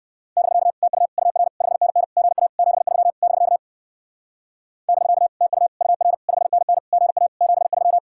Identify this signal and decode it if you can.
Morse code, heard as a keyed tone, about 58 wpm, 700 Hz. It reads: <BK> TU UR 5NN GA 73 <BK> <BK> TU UR 5NN GA 73